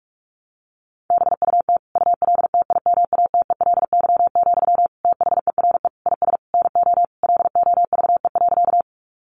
Morse code, heard as a keyed tone, about 44 wpm, 700 Hz. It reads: BUT ULTIMATELY, THERE IS NO LOVE.
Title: Sentences from Top 1200 Words 45wpm